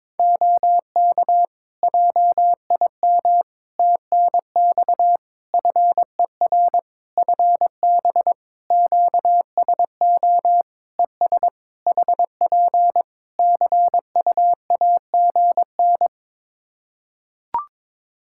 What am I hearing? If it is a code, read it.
Morse code, heard as a keyed tone, about 22 wpm, 700 Hz. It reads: OK JIM TNX FER FB QSO ES HP CUAGN
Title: Practice QSO - Exchange 4 - Ending - 22wpm